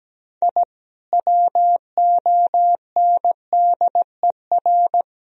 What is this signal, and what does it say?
Morse code, heard as a keyed tone, about 17 wpm, 700 Hz. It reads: I WONDER
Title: Sentences from Top 1000 Words - Review Part 3 - 17wpm